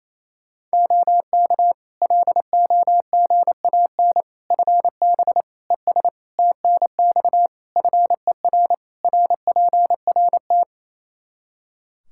Morse code, heard as a keyed tone, about 28 wpm, 700 Hz. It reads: OK LOGAN FB ES TNX FER RPRT
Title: Practice QSO - Exchange 2 - Info - 28wpm